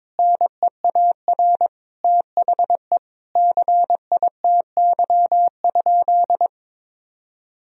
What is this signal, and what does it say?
Morse code, heard as a keyed tone, about 22 wpm, 700 Hz. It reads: NEAR THE CITY?
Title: Sentences from Top 1000 Words - Review Part 3 - 22wpm